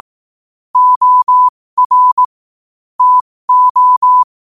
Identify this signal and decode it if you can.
Morse code, heard as a keyed tone, about 18 wpm, spaced as for 15 wpm, 1000 Hz. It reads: OR TO